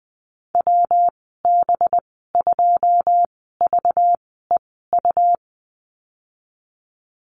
Morse code, heard as a keyed tone, about 19 wpm, spaced as for 14 wpm, 700 Hz. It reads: WB2VEU
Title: 193 Lesson 31g - Letter v - Call Signs 20wpm